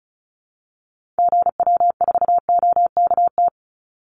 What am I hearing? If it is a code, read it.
Morse code, heard as a keyed tone, about 34 wpm, 700 Hz. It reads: GW4OKT
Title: Call Signs - 35wpm